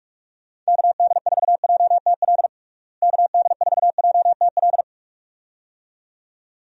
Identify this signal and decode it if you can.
Morse code, heard as a keyed tone, about 46 wpm, 700 Hz. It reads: KD4JTL KD4JTL
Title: USA Call Signs - 6 Characters - Repeated 2x - 45wpm